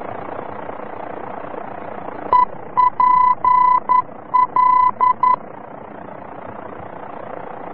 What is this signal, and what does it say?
Morse code, heard as a keyed tone, about 11 wpm, 1000 Hz. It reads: EPL
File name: france / mx / EPL-VOR-F